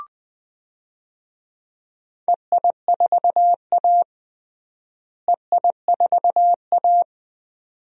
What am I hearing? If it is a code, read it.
Morse code, heard as a keyed tone, about 20 wpm, 700 Hz. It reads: EI4A EI4A
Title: DXCC-100 Call Signs - 4 Characters - Repeated 2x - 20wpm